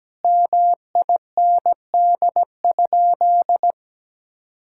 Morse code, heard as a keyed tone, about 17 wpm, 700 Hz. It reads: MIND?
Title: Sentences from Top 1000 Words - Review Part 6 - 17wpm